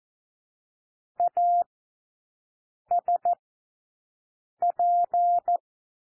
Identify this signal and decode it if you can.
Morse code, heard as a keyed tone, about 14 wpm, spaced as for 6 wpm, 700 Hz. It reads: ASP
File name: cwmp3 / W1 SOUND WORDS 14 wpm 0